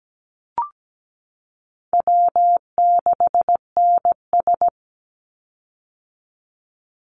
Morse code, heard as a keyed tone, about 17 wpm, 700 Hz. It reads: W6NS
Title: Easy Call Signs - 17wpm